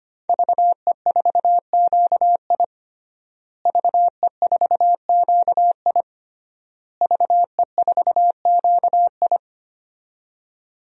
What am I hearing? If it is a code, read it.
Morse code, heard as a keyed tone, about 25 wpm, 700 Hz. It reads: VE4QI VE4QI VE4QI
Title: DXCC-100 Call Signs - 5 Characters - Repeated 3x - 25wpm